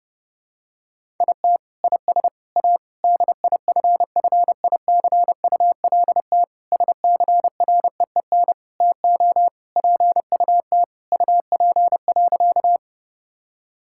Morse code, heard as a keyed tone, about 30 wpm, 700 Hz. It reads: IT IS A DIFFICULT SCREEN TO PUT UP.
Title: Sentences from Top 1000 Words - Review Part 6 - 30wpm